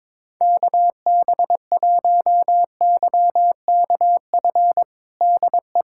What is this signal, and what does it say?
Morse code, heard as a keyed tone, about 22 wpm, 700 Hz. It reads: KB1YKF DE
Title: USA Callsign DE Callsign - 6 Characters - 22wpm